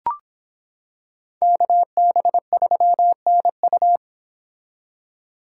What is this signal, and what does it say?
Morse code, heard as a keyed tone, about 26 wpm, 700 Hz. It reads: KB3NU